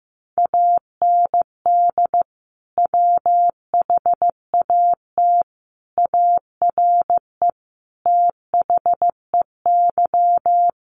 Morse code, heard as a keyed tone, about 15 wpm, 700 Hz. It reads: AND WHAT ARE THEY
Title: Sentences from Top 800 Words 15wpm